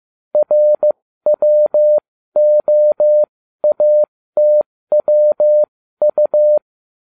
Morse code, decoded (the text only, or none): RWOATWU